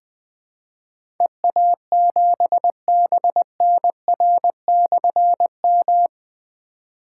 Morse code, heard as a keyed tone, about 20 wpm, 700 Hz. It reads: EA7BNR/M
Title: Difficult Call Signs 20wpm